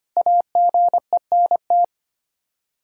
Morse code, heard as a keyed tone, about 25 wpm, 700 Hz. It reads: AGENT
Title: Practice QSO - Exchange 3 - More Info - 25wpm